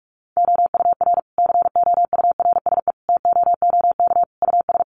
Morse code, heard as a keyed tone, about 44 wpm, 700 Hz. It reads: OUR COURSE TOOK US
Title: Sentences from Top 1300 Words 45wpm